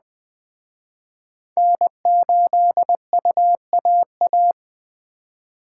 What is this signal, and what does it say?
Morse code, heard as a keyed tone, about 20 wpm, 700 Hz. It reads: N8UAA